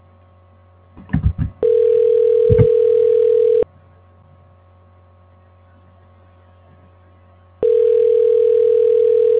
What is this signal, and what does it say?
Ringback tone (North America), 440 + 480 Hz.
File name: recordings / REe1ab27a83c0a b805e84db5006